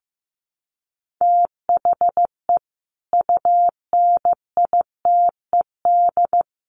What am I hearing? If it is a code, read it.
Morse code, heard as a keyed tone, about 14 wpm, 700 Hz. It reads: THE UNITED
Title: Sets of 5 Words 15wpm